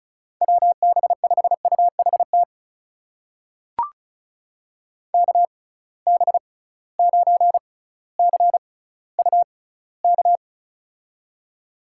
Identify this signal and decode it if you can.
Morse code, heard as a keyed tone, about 34 wpm, spaced as for 18 wpm, 700 Hz. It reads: WB5UHT KB9CUK
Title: USA Call Signs - 6 Characters - 6x Character Spacing - 35wpm